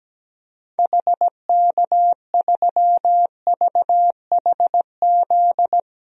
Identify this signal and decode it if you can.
Morse code, heard as a keyed tone, about 17 wpm, 700 Hz. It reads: HK3VHZ
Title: DXCC-100 Call Signs - 6 Characters - Repeated 2x - 17wpm